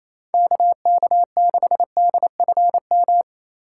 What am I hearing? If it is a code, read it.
Morse code, heard as a keyed tone, about 35 wpm, 700 Hz. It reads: KK6DFM
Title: Speed Racing - USA 2x3 Call Signs 28wpm